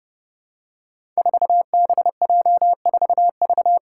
Morse code, heard as a keyed tone, about 30 wpm, 700 Hz. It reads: 4BJ4V
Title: Five Challenging Letter-Number Combinations 30wpm